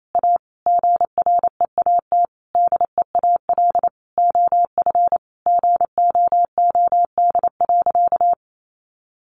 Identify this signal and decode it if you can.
Morse code, heard as a keyed tone, about 28 wpm, 700 Hz. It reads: A GREAT DEAL OF GOOD.
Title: Sentences from Top 900 Words 28wpm